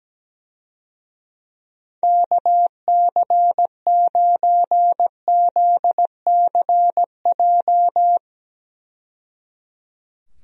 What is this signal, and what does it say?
Morse code, heard as a keyed tone, about 17 wpm, 700 Hz. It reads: KC9ZCJ
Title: USA Call Signs - 6 Characters - 17wpm